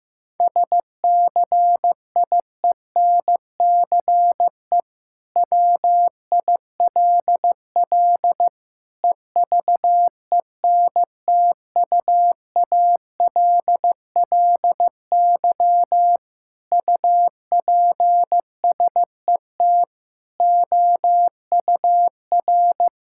Morse code, heard as a keyed tone, about 15 wpm, 700 Hz. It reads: SCIENCE WILL EVENTUALLY UPSET OUR
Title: Sentences from Top 1000 Words - Review Part 6 - 15wpm